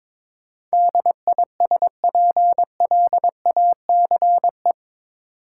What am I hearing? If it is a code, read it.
Morse code, heard as a keyed tone, about 22 wpm, 700 Hz. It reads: DISPLACE
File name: practice / 22wpm / Eight Letter Words - 22wpm